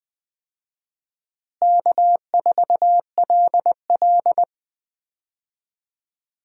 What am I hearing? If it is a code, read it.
Morse code, heard as a keyed tone, about 20 wpm, 700 Hz. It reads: K4LL
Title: USA Call Signs - 4 Characters - Repeated 3x - 20wpm